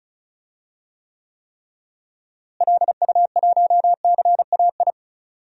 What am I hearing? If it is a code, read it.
Morse code, heard as a keyed tone, about 36 wpm, 700 Hz. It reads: LU1CAI
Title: DXCC-100 Call Signs - 6 Characters - Repeated 3x - 35wpm